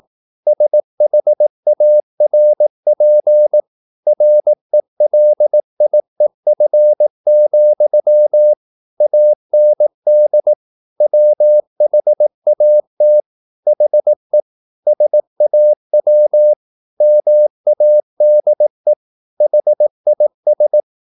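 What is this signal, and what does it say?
Morse code, heard as a keyed tone, about 18 wpm, 600 Hz. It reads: SHARP RELIEF, AND WHAT HE SAW MADE HIS
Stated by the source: Morse practice at 18 wpm, English text, from CW med Gnister.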